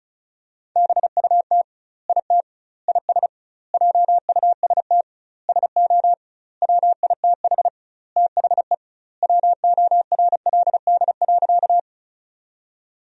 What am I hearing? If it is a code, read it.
Morse code, heard as a keyed tone, about 34 wpm, 700 Hz. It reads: BUT IT IS JUST SO WITH THE WORLD.
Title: 2x Word-Spacing - Sentences from Top 200 Words - 35wpm